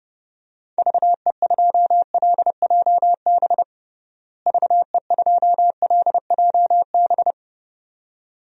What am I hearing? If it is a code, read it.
Morse code, heard as a keyed tone, about 32 wpm, 700 Hz. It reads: VE2LJB VE2LJB
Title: DXCC-100 Call Signs - 6 Characters - Repeated 2x - 30wpm